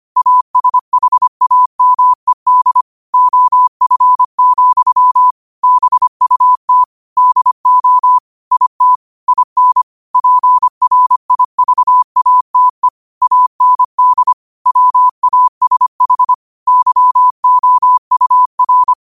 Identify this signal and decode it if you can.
Morse code, heard as a keyed tone, about 25 wpm, 1000 Hz. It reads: ASHAMED OF, BUT DO IT IN PRIVATE AND WASH YOUR